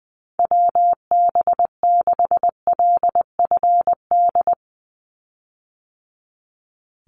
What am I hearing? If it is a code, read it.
Morse code, heard as a keyed tone, about 20 wpm, 700 Hz. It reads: WB6LFD
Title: USA Call Signs 20wpm